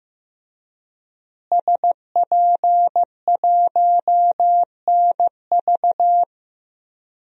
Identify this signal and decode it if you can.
Morse code, heard as a keyed tone, about 15 wpm, 700 Hz. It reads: SP1NV
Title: DXCC-100 Call Signs - 5 Characters - Repeated 3x - 15wpm